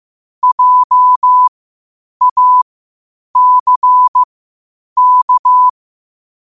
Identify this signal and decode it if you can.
Morse code, heard as a keyed tone, about 15 wpm, spaced as for 8 wpm, 1000 Hz. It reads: JACK
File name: qotdfiles / 8wpm